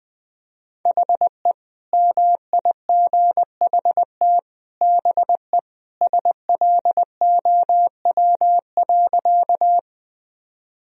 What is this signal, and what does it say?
Morse code, heard as a keyed tone, about 20 wpm, 700 Hz. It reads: HE MIGHT BE SLOW.
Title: Sentences from Top 1000 Words - Review Part 2 - 20wpm